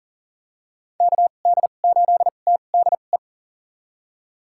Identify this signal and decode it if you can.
Morse code, heard as a keyed tone, about 40 wpm, spaced as for 29 wpm, 700 Hz. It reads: KD8TDE
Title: USA Call Signs - 6 Characters - 2x Character Spacing - 40wpm